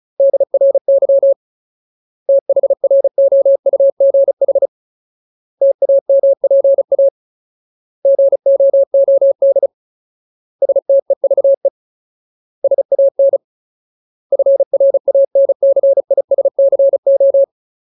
Morse code, heard as a keyed tone, about 34 wpm, 550 Hz. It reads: DRY THROUGH TAMPA GOOD STEVE SAN FRANCISCO
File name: Ultimate Word List / 35WPM Ultimate